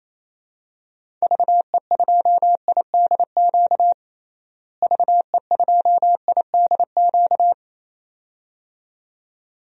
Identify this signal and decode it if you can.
Morse code, heard as a keyed tone, about 28 wpm, 700 Hz. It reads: VE2IDQ VE2IDQ
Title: DXCC-100 Call Signs - 6 Characters - Repeated 2x - 28wpm